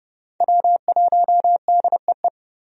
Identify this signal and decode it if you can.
Morse code, heard as a keyed tone, about 30 wpm, 700 Hz. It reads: W1DEE
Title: USA Call Signs - 5 Characters - 30wpm